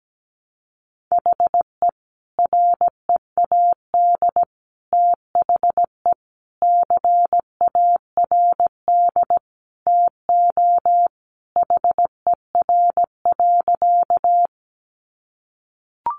Morse code, heard as a keyed tone, about 17 wpm, 700 Hz. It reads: HE READ THE CARD TO HER.
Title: Sentences from Top 600 Words 17wpm